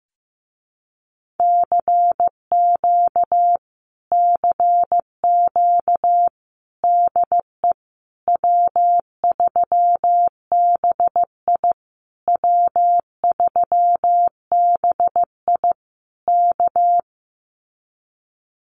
Morse code, heard as a keyed tone, about 15 wpm, 700 Hz. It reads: CQ CQ DE W3BI W3BI K
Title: Common QSO Phrases Encore 15wpm